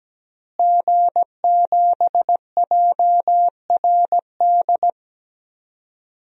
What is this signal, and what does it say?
Morse code, heard as a keyed tone, about 17 wpm, 700 Hz. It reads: G7JRD
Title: DXCC-100 Call Signs - 5 Characters - Repeated 3x - 17wpm